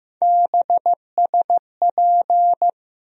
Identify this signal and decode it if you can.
Morse code, heard as a keyed tone, about 15 wpm, 700 Hz. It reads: BSP
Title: USA Call Sign Postfixes - 3 Letters 15wpm